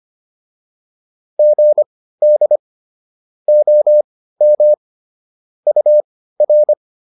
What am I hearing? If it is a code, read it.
Morse code, heard as a keyed tone, about 25 wpm, spaced as for 15 wpm, 600 Hz. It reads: GD OM UR